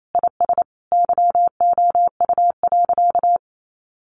Morse code, heard as a keyed tone, about 28 wpm, 700 Hz. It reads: IS YOU.
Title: Sentences from Top 1100 Words 28wpm